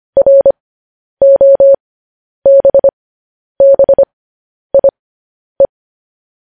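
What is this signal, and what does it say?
Morse code, heard as a keyed tone, about 25 wpm, spaced as for 10 wpm, 550 Hz. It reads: ROBBIE
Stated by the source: News headlines sent in Morse code at 10 wpm.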